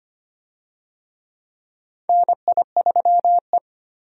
Morse code, heard as a keyed tone, about 25 wpm, 700 Hz. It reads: NI3E
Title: USA Call Signs - 4 Characters - 25wpm